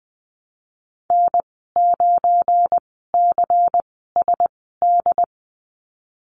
Morse code, heard as a keyed tone, about 19 wpm, spaced as for 14 wpm, 700 Hz. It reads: N9CSD
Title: 173 Lesson 28g - Number 9 - Call Signs 20wpm